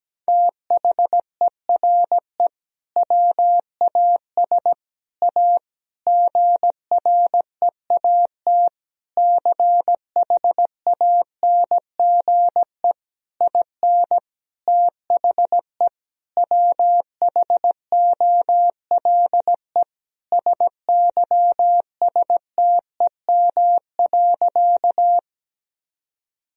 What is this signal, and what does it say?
Morse code, heard as a keyed tone, about 17 wpm, 700 Hz. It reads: THERE WAS A GREAT CHANGE IN THE WHOLE SYSTEM.
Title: Sentences from Top 1000 Words - Review Part 3 - 17wpm